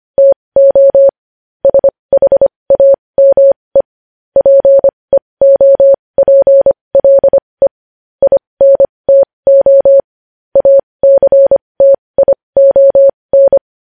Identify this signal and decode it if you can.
Morse code, heard as a keyed tone, about 25 wpm, spaced as for 20 wpm, 550 Hz. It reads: TO SHAME PEOPLE INTO ACTION
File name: podcast / News headlines-1x-25x20